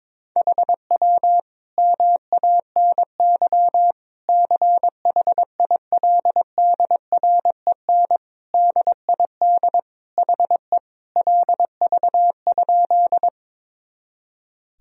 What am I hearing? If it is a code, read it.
Morse code, heard as a keyed tone, about 21 wpm, 700 Hz. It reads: HW MANY CHILDREN DID HE LV?
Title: Sentences from Top 400 Words and Abbreviations 22wpm